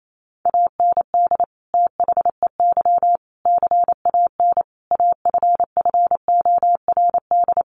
Morse code, heard as a keyed tone, about 28 wpm, 700 Hz. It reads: AND THEY CAN AFFORD
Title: Sentences from Top 1100 Words 28wpm